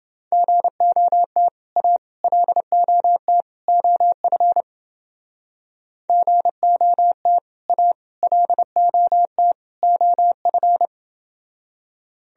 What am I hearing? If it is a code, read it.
Morse code, heard as a keyed tone, about 30 wpm, 700 Hz. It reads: GOT A LOT OF GOT A LOT OF